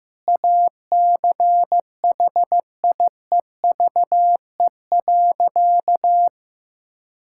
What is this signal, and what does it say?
Morse code, heard as a keyed tone, about 15 wpm, 700 Hz. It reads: ACHIEVE.
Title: Sentences from Top 1000 Words - Review Part 6 - 15wpm